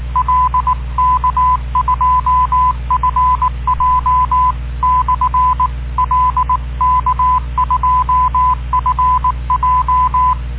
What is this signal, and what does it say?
Morse code, heard as a keyed tone, about 19 wpm, 1010 Hz. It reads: L K2FJ/L K2FJ